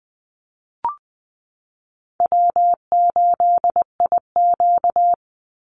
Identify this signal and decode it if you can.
Morse code, heard as a keyed tone, about 20 wpm, 700 Hz. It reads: W8IQ
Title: Call Signs Repeated 3x - 20wpm